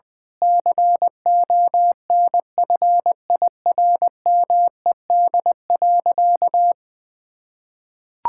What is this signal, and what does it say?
Morse code, heard as a keyed tone, about 20 wpm, 700 Hz. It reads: CONFIRMED.
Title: Sentences with 8-Word Phrases - 20wpm